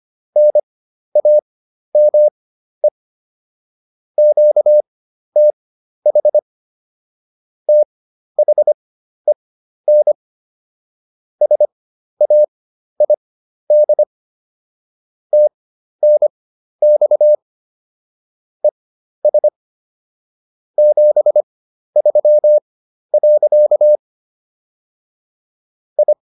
Morse code, heard as a keyed tone, about 25 wpm, spaced as for 12 wpm, 600 Hz. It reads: NAME QTH THEN SAID TNX ES 73. I